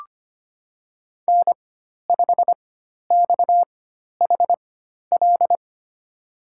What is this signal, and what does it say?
Morse code, heard as a keyed tone, about 25 wpm, spaced as for 12 wpm, 700 Hz. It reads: N5XHL
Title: USA Call Signs - 4x Character Spacing - 25wpm